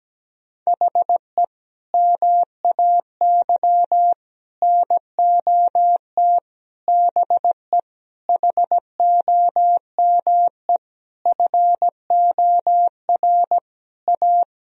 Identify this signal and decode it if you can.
Morse code, heard as a keyed tone, about 17 wpm, 700 Hz. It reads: HE MAY NOT BE HOME FOR A
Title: Sentences from Top 1000 Words - Review Part 2 - 17wpm